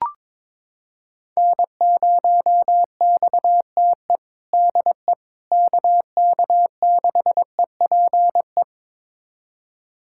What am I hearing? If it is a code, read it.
Morse code, heard as a keyed tone, about 22 wpm, 700 Hz. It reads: N0XTE DE KK6EPE
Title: USA Callsign DE Callsign 22wpm